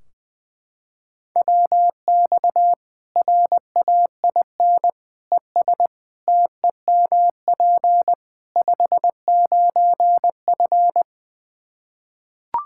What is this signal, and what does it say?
Morse code, heard as a keyed tone, about 20 wpm, 700 Hz. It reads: WX RAIN ES TEMP 59F
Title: Practice QSO - Exchange 2 - Info - 20wpm